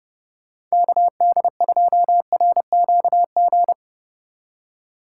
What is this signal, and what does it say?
Morse code, heard as a keyed tone, about 40 wpm, 700 Hz. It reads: KD2RQG